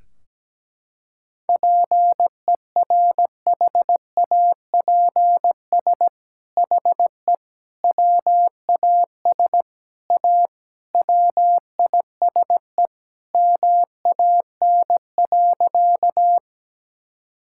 Morse code, heard as a keyed tone, about 17 wpm, 700 Hz. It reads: PERHAPS HE WAS A WISE MAN.
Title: Sentences from Top 1600 Words 17wpm